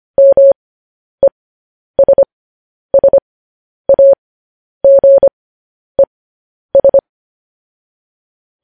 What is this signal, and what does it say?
Morse code, heard as a keyed tone, about 25 wpm, spaced as for 10 wpm, 550 Hz. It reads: MESSAGES
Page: news headlines in morse code at 10 wpm